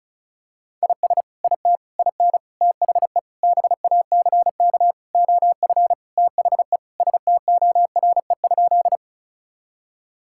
Morse code, heard as a keyed tone, about 34 wpm, 700 Hz. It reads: IS IT IN THE BACK OF THE STORE?
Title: Sentences from Top 1000 Words - Review Part 5 - 35wpm